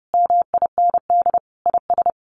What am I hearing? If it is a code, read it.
Morse code, heard as a keyed tone, about 30 wpm, 700 Hz. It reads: MIND IS
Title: Sentences from Top 1500 Words 30wpm